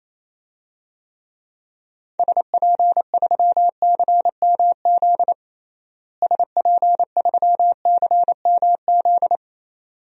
Morse code, heard as a keyed tone, about 28 wpm, 700 Hz. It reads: SP3CMZ SP3CMZ
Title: DXCC-100 Call Signs - 6 Characters - Repeated 2x - 28wpm